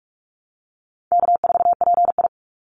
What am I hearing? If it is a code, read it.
Morse code, heard as a keyed tone, about 44 wpm, 700 Hz. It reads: K4PI
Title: Easy Call Signs Repeated 3x - 45wpm